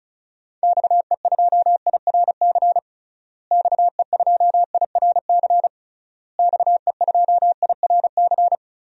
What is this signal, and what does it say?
Morse code, heard as a keyed tone, about 36 wpm, 700 Hz. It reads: XE2IRC XE2IRC XE2IRC
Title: DXCC-100 Call Signs - 6 Characters - Repeated 3x - 35wpm